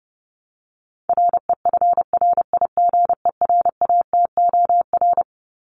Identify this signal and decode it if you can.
Morse code, heard as a keyed tone, about 30 wpm, 700 Hz. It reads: REFRIGERATOR
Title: Top 1501-1750 Words 30wpm